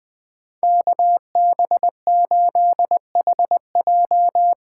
Morse code, heard as a keyed tone, about 20 wpm, 700 Hz. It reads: KB8HJ
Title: USA Call Signs - 6 Characters - 8x Character Spacing - 20wpm